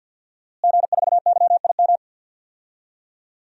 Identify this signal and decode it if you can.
Morse code, heard as a keyed tone, about 52 wpm, 700 Hz. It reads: G4YIM
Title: DXCC-100 Call Signs - 5 Characters - Repeated 3x - 50wpm